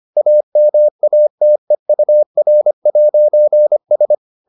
Morse code, heard as a keyed tone, about 25 wpm, 600 Hz. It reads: AMATEUR'S